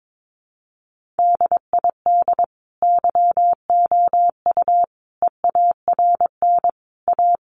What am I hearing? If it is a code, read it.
Morse code, heard as a keyed tone, about 22 wpm, 700 Hz. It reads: DID YOU EARN A
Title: Sentences from Top 1500 Words 22wpm